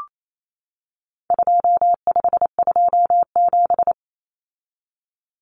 Morse code, heard as a keyed tone, about 28 wpm, 700 Hz. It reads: 2527
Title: Numbers - 4-Digit 28wpm